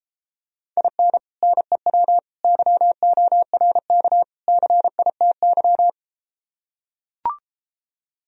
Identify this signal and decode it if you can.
Morse code, heard as a keyed tone, about 33 wpm, 700 Hz. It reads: IN NEW YORK CITY